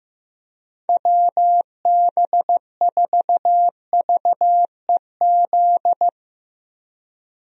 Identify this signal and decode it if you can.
Morse code, heard as a keyed tone, about 16 wpm, 700 Hz. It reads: WB4VEZ
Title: USA Call Signs - 6 Characters - Repeated 2x - 15wpm